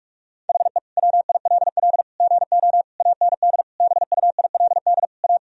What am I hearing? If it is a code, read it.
Morse code, heard as a keyed tone, about 42 wpm, 700 Hz. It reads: HE WILL GO AND BUILD A
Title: Sentences from Top 1000 Words - Review Part 1 - 45wpm